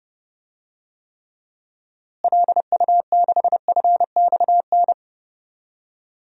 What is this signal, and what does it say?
Morse code, heard as a keyed tone, about 30 wpm, 700 Hz. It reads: LU6FXN
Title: DXCC-100 Call Signs - 6 Characters - Repeated 2x - 30wpm